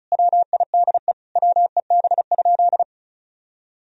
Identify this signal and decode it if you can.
Morse code, heard as a keyed tone, about 36 wpm, 700 Hz. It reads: WIDE WEB?
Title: Sentences from Top 1000 Words - Review Part 4 - 35wpm